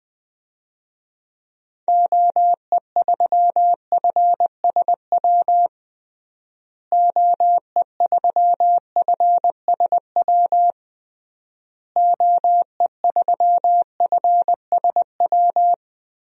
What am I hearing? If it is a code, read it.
Morse code, heard as a keyed tone, about 21 wpm, 700 Hz. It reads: OE3FSW OE3FSW OE3FSW
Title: DXCC-100 Call Signs - 6 Characters - Repeated 3x - 20wpm